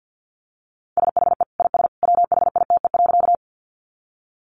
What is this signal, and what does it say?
Morse code, heard as a keyed tone, about 50 wpm, 700 Hz. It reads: SHE IS WHITE.
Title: Easy Phrases 50wpm